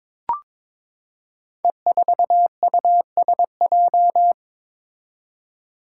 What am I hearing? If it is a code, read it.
Morse code, heard as a keyed tone, about 22 wpm, 700 Hz. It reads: E4USJ
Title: Five Challenging Letter-Number Combinations 22wpm